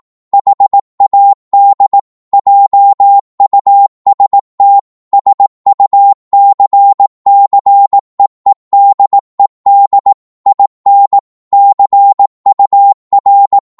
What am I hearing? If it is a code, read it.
Morse code, heard as a keyed tone, about 18 wpm, 800 Hz. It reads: HAD JUST SUCCEEDED IN CUR
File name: 18wpm